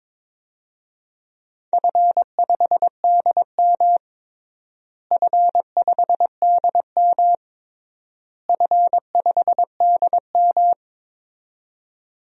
Morse code, heard as a keyed tone, about 22 wpm, 700 Hz. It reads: F5DM F5DM F5DM
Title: DXCC-100 Call Signs - 4 Characters - Repeated 3x - 22wpm